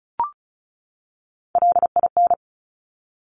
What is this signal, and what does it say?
Morse code, heard as a keyed tone, about 36 wpm, 700 Hz. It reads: LIN
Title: Top Three Letter Combinations in Words 35wpm